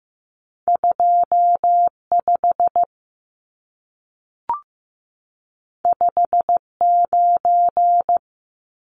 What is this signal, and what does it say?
Morse code, heard as a keyed tone, about 15 wpm, 700 Hz. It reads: 25 59
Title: Numbers - 2-Digit 15wpm